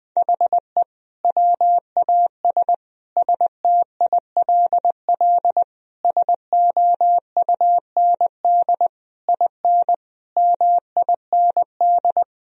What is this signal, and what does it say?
Morse code, heard as a keyed tone, about 20 wpm, 700 Hz. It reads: HE WAS STILL SOUND IN MIND
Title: Sentences from Top 1000 Words - Review Part 2 - 20wpm